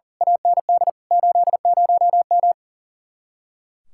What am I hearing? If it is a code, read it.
Morse code, heard as a keyed tone, about 42 wpm, 700 Hz. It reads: AND 80M
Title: Speed-Racing Common QSO Phrases - 40wpm